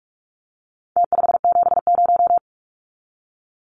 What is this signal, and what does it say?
Morse code, heard as a keyed tone, about 44 wpm, 700 Hz. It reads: T570
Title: Numbers - Contest Style - 4-Digit 45wpm